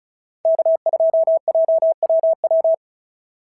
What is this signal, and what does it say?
Morse code, heard as a keyed tone, about 35 wpm, 650 Hz. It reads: K2JWW
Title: SOTA QSO Practice 35wpm